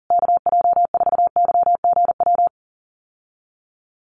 Morse code, heard as a keyed tone, about 40 wpm, 700 Hz. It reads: KJ4YGW
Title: USA Call Signs Repeated 3x 40wpm